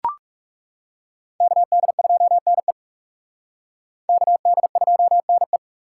Morse code, heard as a keyed tone, about 37 wpm, 700 Hz. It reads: KD2NE KD2NE
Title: Speed Racing - USA 2x2 Call Signs 30wpm